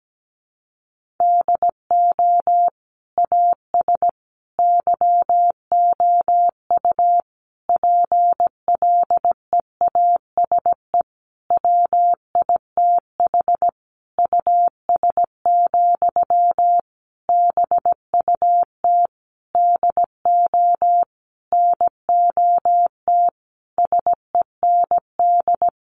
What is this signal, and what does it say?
Morse code, heard as a keyed tone, about 17 wpm, 700 Hz. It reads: DO AS YOU PLEASE WITH US, BUT DO NOT SEND
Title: Sentences from Top 1500 Words 17wpm